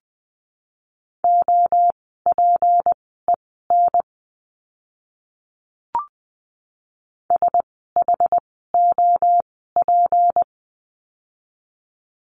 Morse code, heard as a keyed tone, about 20 wpm, spaced as for 14 wpm, 700 Hz. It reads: OPEN SHOP
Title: 153 Lesson 25f - Letter p - Words 20wpm